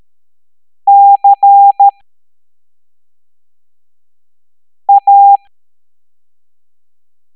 Morse code, heard as a keyed tone, about 13 wpm, spaced as for 3 wpm, 800 Hz. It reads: CA
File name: hamdata / L09-EX04